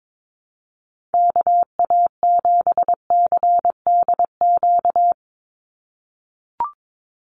Morse code, heard as a keyed tone, about 22 wpm, 700 Hz. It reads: KA7CDQ
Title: USA Call Signs 22wpm